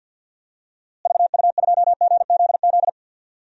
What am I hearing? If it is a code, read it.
Morse code, heard as a keyed tone, about 46 wpm, 700 Hz. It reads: VU2GZZ
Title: DXCC-100 Call Signs - 6 Characters - Repeated 3x - 50wpm